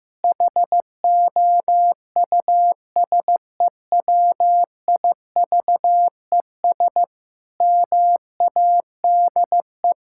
Morse code, heard as a keyed tone, about 15 wpm, 700 Hz. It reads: HOUSEWIVES MADE
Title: Pangram Challenge v2 - 15wpm